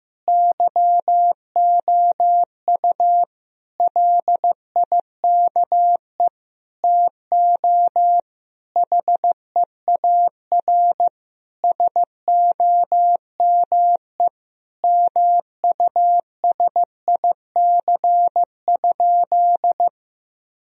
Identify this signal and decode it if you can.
Morse code, heard as a keyed tone, about 15 wpm, 700 Hz. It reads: YOU LIKE TO HEAR SOME MUSIC?
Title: Sentences from Top 1000 Words - Review Part 1 - 15wpm